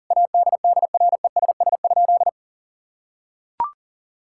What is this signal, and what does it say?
Morse code, heard as a keyed tone, about 40 wpm, 700 Hz. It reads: ADDRESS?
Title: Sentences from Top 1000 Words - Review Part 6 - 40wpm